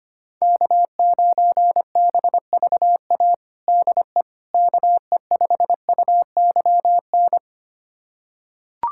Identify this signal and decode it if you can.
Morse code, heard as a keyed tone, about 25 wpm, 700 Hz. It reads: K9BVA DE KE5UYN
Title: USA Callsign DE Callsign 25wpm